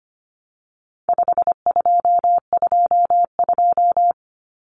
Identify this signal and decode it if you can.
Morse code, heard as a keyed tone, about 25 wpm, 700 Hz. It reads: 5222
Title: Numbers - 4-Digit 25wpm